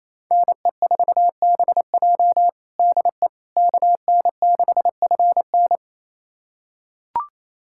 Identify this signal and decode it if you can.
Morse code, heard as a keyed tone, about 28 wpm, 700 Hz. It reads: NE4BJ DE KN6FN
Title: USA Callsign DE Callsign - 5 Characters - 28wpm